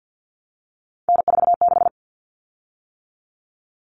Morse code, heard as a keyed tone, about 48 wpm, 700 Hz. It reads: N46
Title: Common Cut Numbers - 3-Digit 50wpm